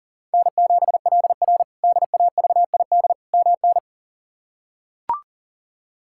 Morse code, heard as a keyed tone, about 40 wpm, 700 Hz. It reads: N7LR DAVID MN
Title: K1USN SST QSO Practice - 40wpm